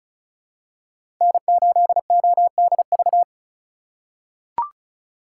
Morse code, heard as a keyed tone, about 34 wpm, 700 Hz. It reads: N8ODV
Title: USA Call Signs - 5 Characters - 4x Character Spacing - 35wpm